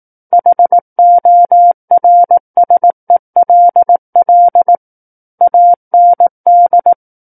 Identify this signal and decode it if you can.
Morse code, heard as a keyed tone, about 18 wpm, 700 Hz. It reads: HORSELL AND
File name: 18 WPM